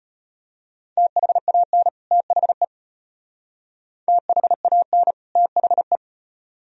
Morse code, heard as a keyed tone, about 32 wpm, 700 Hz. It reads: THAN THE THAN THE